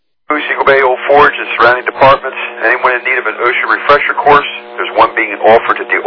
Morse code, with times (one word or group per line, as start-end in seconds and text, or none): none